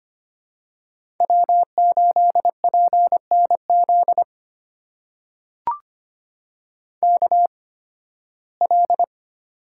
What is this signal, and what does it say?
Morse code, heard as a keyed tone, about 25 wpm, spaced as for 16 wpm, 700 Hz. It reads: W8PNZ KL
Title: USA Call Signs - 5 Characters - 8x Character Spacing - 25wpm